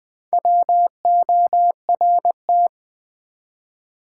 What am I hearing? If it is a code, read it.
Morse code, heard as a keyed tone, about 20 wpm, 700 Hz. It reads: WORT
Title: Build A Word - 4 Letters At A Time - 10-Letter Words - 20wpm